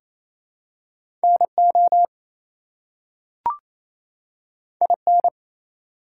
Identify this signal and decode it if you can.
Morse code, heard as a keyed tone, about 27 wpm, 700 Hz. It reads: NO IN
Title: Two Letter Words - 28wpm